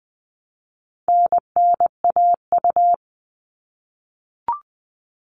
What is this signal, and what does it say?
Morse code, heard as a keyed tone, about 20 wpm, 700 Hz. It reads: NNAU
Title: All Cut Numbers - 4-Digit 20wpm